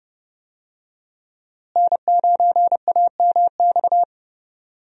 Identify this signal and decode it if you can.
Morse code, heard as a keyed tone, about 30 wpm, 700 Hz. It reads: N9AMX
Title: USA Call Signs - 5 Characters - Repeated 3x - 30wpm